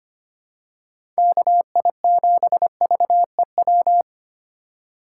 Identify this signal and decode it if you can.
Morse code, heard as a keyed tone, about 25 wpm, 700 Hz. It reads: KI7VEW
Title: USA Call Signs - 6 Characters - 3x Character Spacing - 25wpm